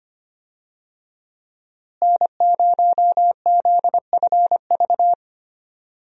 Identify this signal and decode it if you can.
Morse code, heard as a keyed tone, about 25 wpm, 700 Hz. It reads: N0ZFV
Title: USA Call Signs - 5 Characters - Repeated 2x - 25wpm